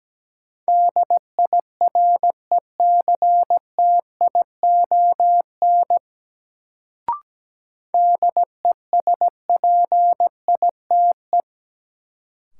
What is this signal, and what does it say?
Morse code, heard as a keyed tone, about 17 wpm, 700 Hz. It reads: DIRECTION DESPITE
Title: Sets of 4 Words - New Word Drill - 17wpm